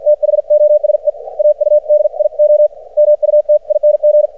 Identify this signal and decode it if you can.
Morse code, heard as a keyed tone, about 53 wpm, spaced as for 46 wpm, 600 Hz. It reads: THOSE AUDIO MUTING
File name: archived / CRSnet